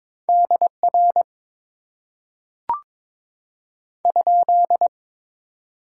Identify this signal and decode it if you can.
Morse code, heard as a keyed tone, about 22 wpm, 700 Hz. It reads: DR ?
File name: Top 90 QSO Elements - 8x Character Spacing - 22wpm